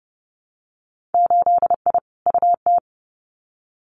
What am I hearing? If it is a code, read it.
Morse code, heard as a keyed tone, about 30 wpm, 700 Hz. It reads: 8I UT
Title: Winter Field Day Exchanges Encore 30wpm